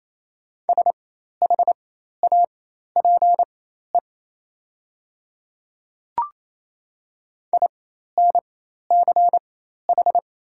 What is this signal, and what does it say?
Morse code, heard as a keyed tone, about 28 wpm, spaced as for 13 wpm, 700 Hz. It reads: SHAPE INCH
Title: 4x Character Spacing - Top 500 Words - 28wpm